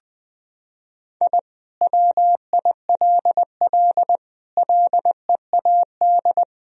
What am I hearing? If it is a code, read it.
Morse code, heard as a keyed tone, about 20 wpm, 700 Hz. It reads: I WILL LEAD
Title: Sentences from Top 1000 Words - Review Part 2 - 20wpm